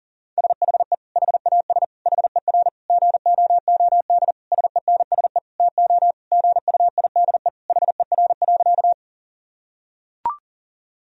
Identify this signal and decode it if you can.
Morse code, heard as a keyed tone, about 40 wpm, 700 Hz. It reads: SHE HAS HER GOOD SENSE TO GUIDE HER.
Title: Sentences from Top 1000 Words - Review Part 4 - 40wpm